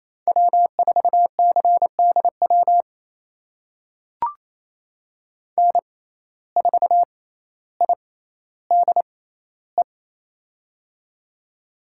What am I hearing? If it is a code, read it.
Morse code, heard as a keyed tone, about 28 wpm, spaced as for 14 wpm, 700 Hz. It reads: W4CDW N4IDE
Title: USA Call Signs - 5 Characters - 6x Character Spacing - 28wpm